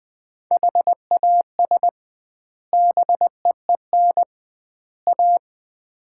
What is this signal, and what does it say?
Morse code, heard as a keyed tone, about 21 wpm, 700 Hz. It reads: HAS BEEN A
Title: 2x Word-Spacing - Sets of 3 Words - 20wpm